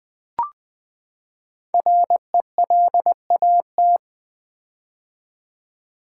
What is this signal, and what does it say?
Morse code, heard as a keyed tone, about 20 wpm, 700 Hz. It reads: RELAT
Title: Build A Word - 5 Letters At A Time - 10-Letter Words - 20wpm